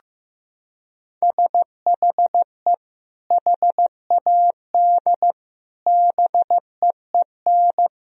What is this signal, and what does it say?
Morse code, heard as a keyed tone, about 15 wpm, 700 Hz. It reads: SHE HAD BEEN
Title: Sentences from Top 1000 Words - Review Part 4 - 15wpm